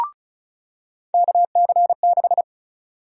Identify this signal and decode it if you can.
Morse code, heard as a keyed tone, about 36 wpm, 700 Hz. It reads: KC6
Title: USA Call Sign Prefixes - 35wpm